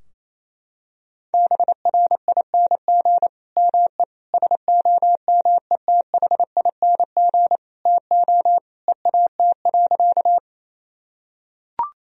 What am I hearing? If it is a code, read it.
Morse code, heard as a keyed tone, about 28 wpm, 700 Hz. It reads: BRING ME SOMETHING TO EAT.
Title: Sentences from Top 1000 Words - Review Part 3 - 28wpm